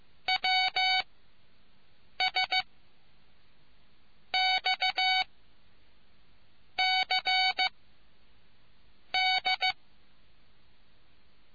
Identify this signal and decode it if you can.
Morse code, heard as a keyed tone, about 15 wpm, spaced as for 5 wpm, 750 Hz. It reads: WSXCD